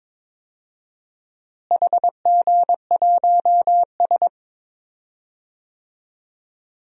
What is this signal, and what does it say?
Morse code, heard as a keyed tone, about 22 wpm, 700 Hz. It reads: HG1S
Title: DXCC-100 Call Signs - 4 Characters - Repeated 2x - 22wpm